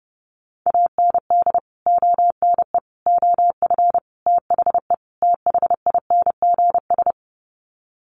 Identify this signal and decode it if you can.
Morse code, heard as a keyed tone, about 30 wpm, 700 Hz. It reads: AND ONE OF THE THINGS
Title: Sets of 5 Words 30wpm